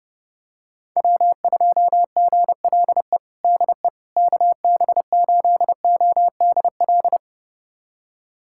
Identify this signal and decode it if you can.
Morse code, heard as a keyed tone, about 32 wpm, 700 Hz. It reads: W2GLE DE KB8ODL
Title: USA Callsign DE Callsign 30wpm